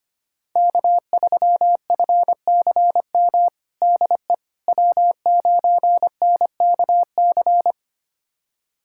Morse code, heard as a keyed tone, about 25 wpm, 700 Hz. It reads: K3FCM DE W9NKC
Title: USA Callsign DE Callsign - 5 Characters - 25wpm